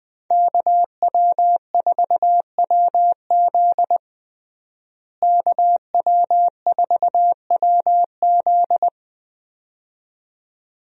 Morse code, heard as a keyed tone, about 20 wpm, 700 Hz. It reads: KW4WZ KW4WZ
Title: USA Call Signs - 5 Characters - Repeated 2x - 20wpm